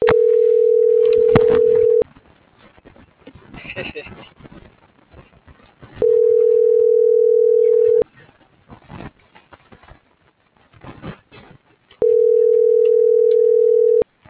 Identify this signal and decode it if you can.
Ringback tone (North America), 440 + 480 Hz.